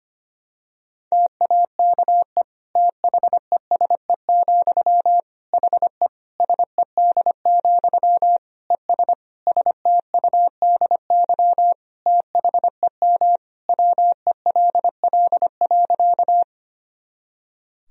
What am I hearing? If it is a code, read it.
Morse code, heard as a keyed tone, about 25 wpm, 700 Hz. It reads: TAKE THESE, HE SED, ES STUDY THEM WELL.
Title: Sentences from Top 300 Words and Abbreviations 25wpm